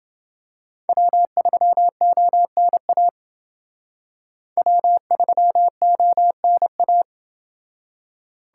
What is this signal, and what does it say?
Morse code, heard as a keyed tone, about 28 wpm, 700 Hz. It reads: W3ONA W3ONA